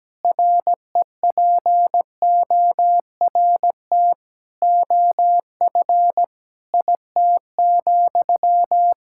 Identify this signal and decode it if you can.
Morse code, heard as a keyed tone, about 17 wpm, 700 Hz. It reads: REPORT OF IT,
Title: Sentences from Top 1000 Words - Review Part 6 - 17wpm